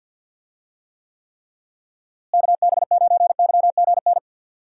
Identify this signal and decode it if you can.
Morse code, heard as a keyed tone, about 52 wpm, 700 Hz. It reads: KB9YGN
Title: USA Call Signs - 6 Characters - Repeated 3x - 50wpm